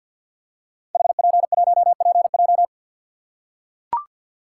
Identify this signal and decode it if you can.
Morse code, heard as a keyed tone, about 45 wpm, 700 Hz. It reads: HP1PJ
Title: Code Groups - 5 Letter:Numbers - 50wpm